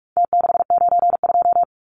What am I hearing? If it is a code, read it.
Morse code, heard as a keyed tone, about 44 wpm, 700 Hz. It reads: T692
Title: Numbers - Contest Style - 4-Digit 45wpm